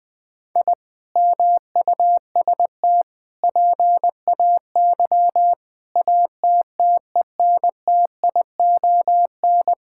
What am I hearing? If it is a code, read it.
Morse code, heard as a keyed tone, about 19 wpm, 700 Hz. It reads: I MUST PAY ATTENTION
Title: Sentences from Top 1000 Words - Review Part 6 - 20wpm